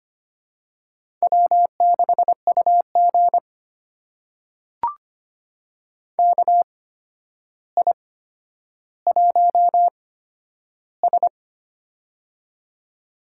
Morse code, heard as a keyed tone, about 25 wpm, spaced as for 11 wpm, 700 Hz. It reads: W6UG KI1S
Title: USA Call Signs - 4 Characters - 8x Character Spacing - 25wpm